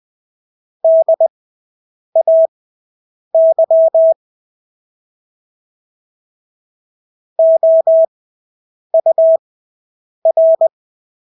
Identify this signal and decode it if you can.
Morse code, heard as a keyed tone, about 20 wpm, spaced as for 8 wpm, 650 Hz. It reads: DAY OUR
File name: Engelske ord 20/8 wpm